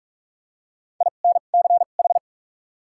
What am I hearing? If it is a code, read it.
Morse code, heard as a keyed tone, about 46 wpm, spaced as for 33 wpm, 700 Hz. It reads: INCH
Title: 2x Character Spacing - Top 500 Words - 45wpm